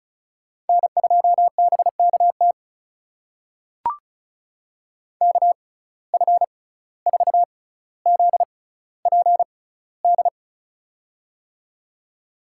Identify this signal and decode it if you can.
Morse code, heard as a keyed tone, about 36 wpm, spaced as for 17 wpm, 700 Hz. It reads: N2BKT KF4ZPD
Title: USA Call Signs - 6x Character Spacing - 35wpm